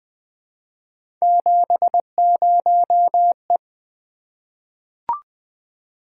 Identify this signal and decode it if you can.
Morse code, heard as a keyed tone, about 20 wpm, 700 Hz. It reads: 70E
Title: Three Letter-Number Combinations - 20wpm